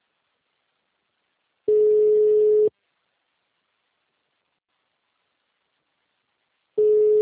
Ringback tone (Europe, 425 Hz).
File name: out- -2003- - - .82